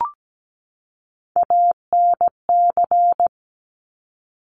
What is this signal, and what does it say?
Morse code, heard as a keyed tone, about 17 wpm, 700 Hz. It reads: ANC
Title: Top Three Letter Combinations in Words 17wpm